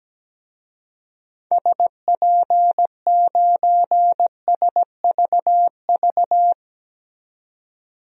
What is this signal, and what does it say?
Morse code, heard as a keyed tone, about 17 wpm, 700 Hz. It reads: SP9SVV
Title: DXCC-100 Call Signs - 6 Characters - 17wpm